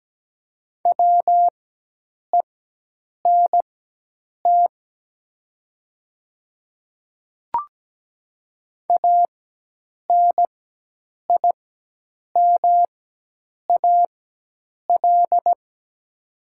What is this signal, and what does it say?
Morse code, heard as a keyed tone, about 17 wpm, spaced as for 8 wpm, 700 Hz. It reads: WENT ANIMAL
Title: 4x Character Spacing - Top 300 Words - 17wpm